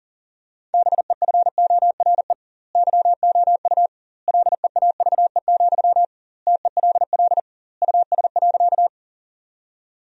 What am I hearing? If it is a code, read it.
Morse code, heard as a keyed tone, about 37 wpm, 700 Hz. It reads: BEFORE YOU LEAVE, TELL US.
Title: 2x Word-Spacing - Sentences from Top 400 Words - 40wpm